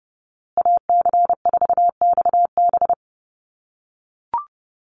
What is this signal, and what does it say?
Morse code, heard as a keyed tone, about 30 wpm, 700 Hz. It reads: AC4XB
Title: USA Call Signs Repeated 2x 30wpm